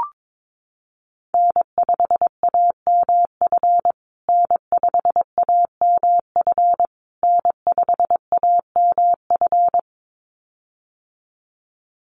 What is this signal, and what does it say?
Morse code, heard as a keyed tone, about 22 wpm, 700 Hz. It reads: N5AMF N5AMF N5AMF
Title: USA Call Signs Repeated 3x 22wpm